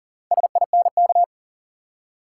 Morse code, heard as a keyed tone, about 40 wpm, 700 Hz. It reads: SINK
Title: Four Letter Words - 40wpm